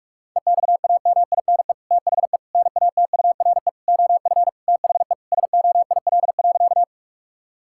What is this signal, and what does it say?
Morse code, heard as a keyed tone, about 46 wpm, 700 Hz. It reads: EXAMINE THE NATURE OF THE SOIL.
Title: Sentences from Top 1600 Words 45wpm